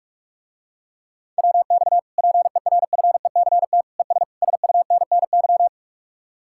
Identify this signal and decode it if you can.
Morse code, heard as a keyed tone, about 44 wpm, 700 Hz. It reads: WX PERFECT ES SUNNY